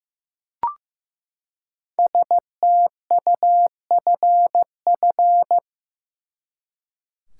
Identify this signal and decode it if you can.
Morse code, heard as a keyed tone, about 15 wpm, 700 Hz. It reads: STUFF
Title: Five Letter Words - 15wpm